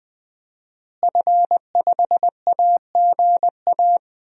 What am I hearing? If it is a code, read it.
Morse code, heard as a keyed tone, about 20 wpm, 700 Hz. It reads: F5AGA
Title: DXCC-100 Call Signs - 5 Characters - Repeated 2x - 20wpm